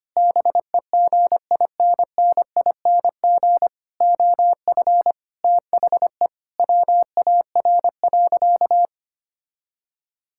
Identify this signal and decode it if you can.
Morse code, heard as a keyed tone, about 25 wpm, 700 Hz. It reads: BEGINNING OF THE WAR.
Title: Sentences from Top 1000 Words - Review Part 6 - 25wpm